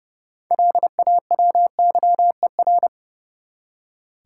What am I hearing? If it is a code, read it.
Morse code, heard as a keyed tone, about 30 wpm, 700 Hz. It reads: LAWYER
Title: Six Letter Words - 30wpm